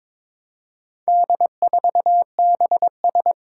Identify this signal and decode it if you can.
Morse code, heard as a keyed tone, about 22 wpm, 700 Hz. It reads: D4BS
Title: Four Challenging Letter-Number Combinations 22wpm